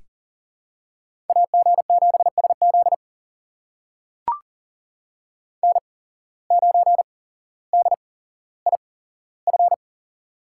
Morse code, heard as a keyed tone, about 40 wpm, spaced as for 17 wpm, 700 Hz. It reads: AG7SZ N9DIF
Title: USA Call Signs - 5 Characters - 8x Character Spacing - 40wpm